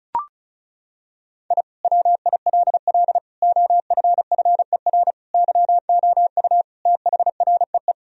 Morse code, heard as a keyed tone, about 34 wpm, 700 Hz. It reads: I WILL OFFER YOU THREE
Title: Sentences from Top 1000 Words - Review Part 4 - 35wpm